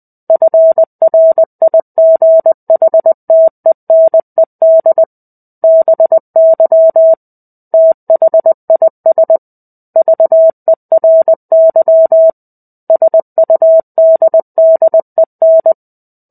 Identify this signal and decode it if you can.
Morse code, heard as a keyed tone, about 20 wpm, 650 Hz. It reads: FRIGHTENED BY THIS VERY SUDDEN
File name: 05-Chapter-05 20 WPM